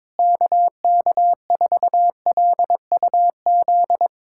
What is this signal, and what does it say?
Morse code, heard as a keyed tone, about 22 wpm, 700 Hz. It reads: KK4LUZ
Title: USA Callsign DE Callsign 22wpm